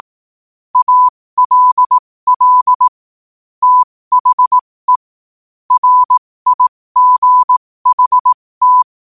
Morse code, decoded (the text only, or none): ALL THE RIGHT